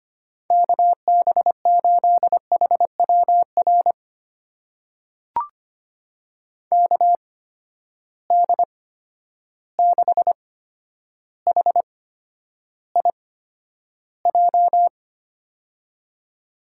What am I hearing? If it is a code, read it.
Morse code, heard as a keyed tone, about 25 wpm, spaced as for 11 wpm, 700 Hz. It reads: KB8HWR KD6HIJ
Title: USA Call Signs - 6 Characters - 8x Character Spacing - 25wpm